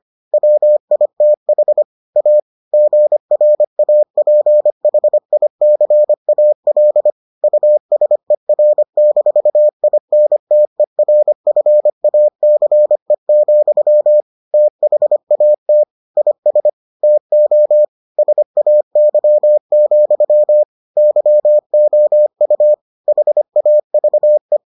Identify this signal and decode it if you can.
Morse code, heard as a keyed tone, about 25 wpm, 600 Hz. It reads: WITH A GRAPHICAL USER-INTERFACE, THAT IS TO SAY, YOU HAVE